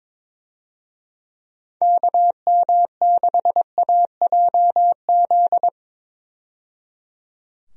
Morse code, heard as a keyed tone, about 21 wpm, 700 Hz. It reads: KM6AJZ
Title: USA Call Signs - 6 Characters - 22wpm